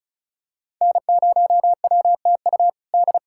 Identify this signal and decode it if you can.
Morse code, heard as a keyed tone, about 34 wpm, 700 Hz. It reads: N0WTU D
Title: USA Callsign DE Callsign - 5 Characters - 35wpm